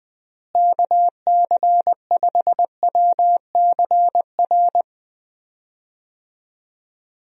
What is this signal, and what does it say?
Morse code, heard as a keyed tone, about 20 wpm, 700 Hz. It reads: KC5WCR
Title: USA Call Signs - 6 Characters - Repeated 2x - 20wpm